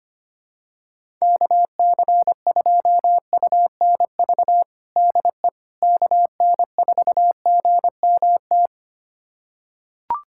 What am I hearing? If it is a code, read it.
Morse code, heard as a keyed tone, about 25 wpm, 700 Hz. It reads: KC2UNV DE KN4GMT
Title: USA Callsign DE Callsign - 6 Characters - 25wpm